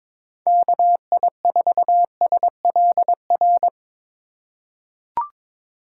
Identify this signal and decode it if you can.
Morse code, heard as a keyed tone, about 22 wpm, 700 Hz. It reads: KI4SLR
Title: USA Call Signs - 6 Characters - 2x Character Spacing - 22wpm